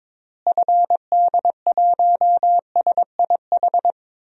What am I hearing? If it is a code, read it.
Morse code, heard as a keyed tone, about 22 wpm, 700 Hz. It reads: FD1SIH
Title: DXCC-100 Call Signs - 6 Characters - Repeated 3x - 22wpm